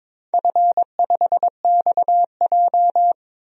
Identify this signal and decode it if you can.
Morse code, heard as a keyed tone, about 22 wpm, 700 Hz. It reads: F5XJ
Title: DXCC-100 Call Signs - 4 Characters - 22wpm